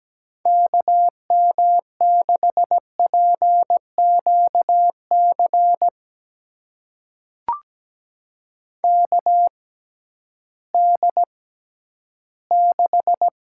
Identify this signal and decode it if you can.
Morse code, heard as a keyed tone, about 17 wpm, spaced as for 11 wpm, 700 Hz. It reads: KM6PQC KD6
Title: USA Call Signs - 6 Characters - 6x Character Spacing - 17wpm